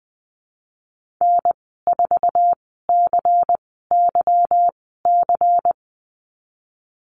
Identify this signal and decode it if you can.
Morse code, heard as a keyed tone, about 20 wpm, spaced as for 15 wpm, 700 Hz. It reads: N4CYC
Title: 173 Lesson 28g - Number 9 - Call Signs 20wpm